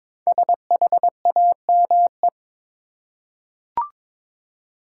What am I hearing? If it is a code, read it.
Morse code, heard as a keyed tone, about 22 wpm, 700 Hz. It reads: SHAME
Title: Five Letter Words - 22wpm